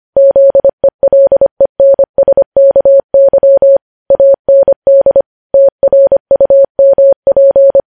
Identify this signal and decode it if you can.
Morse code, heard as a keyed tone, about 25 wpm, 550 Hz. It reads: ZELENSKY AND TRUMP
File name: podcast / News headlines-1x-25x25